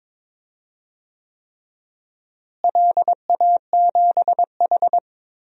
Morse code, heard as a keyed tone, about 22 wpm, 700 Hz. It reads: LA7H
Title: DXCC-100 Call Signs - 4 Characters - Repeated 3x - 22wpm